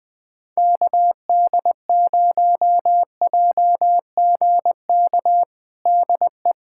Morse code, heard as a keyed tone, about 21 wpm, 700 Hz. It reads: KD0JGK DE
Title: USA Callsign DE Callsign - 6 Characters - 20wpm